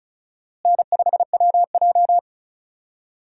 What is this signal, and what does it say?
Morse code, heard as a keyed tone, about 34 wpm, 700 Hz. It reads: N5WJ
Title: USA Call Signs - 4 Characters - Repeated 2x - 35wpm